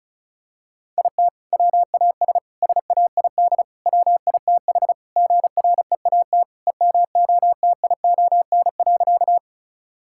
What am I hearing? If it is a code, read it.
Morse code, heard as a keyed tone, about 36 wpm, 700 Hz. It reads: IT WAS SAID WITH GREAT EMOTION.
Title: Sentences from Top 1600 Words 35wpm